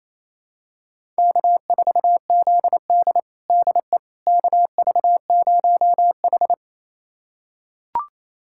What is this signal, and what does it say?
Morse code, heard as a keyed tone, about 28 wpm, 700 Hz. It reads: K4ZD DE KV0H
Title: USA Callsign DE Callsign - 4 Characters - 28wpm